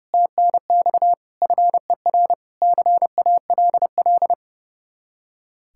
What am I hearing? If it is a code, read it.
Morse code, heard as a keyed tone, about 40 wpm, 700 Hz. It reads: TNX FER CALL